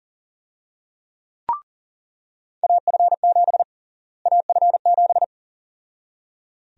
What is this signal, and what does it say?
Morse code, heard as a keyed tone, about 40 wpm, 700 Hz. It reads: AF7 AF7
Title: USA Call Sign Prefixes - Repeated 2x - 40wpm